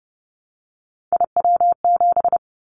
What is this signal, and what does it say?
Morse code, heard as a keyed tone, about 30 wpm, 700 Hz. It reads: IW7
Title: Callsign Prefixes for Top 25 DXCC - 30wpm